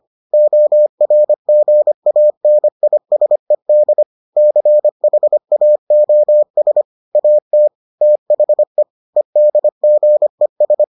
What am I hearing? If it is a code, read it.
Morse code, heard as a keyed tone, about 25 wpm, 600 Hz. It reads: ORGANISED CHAOS AT THE EDGES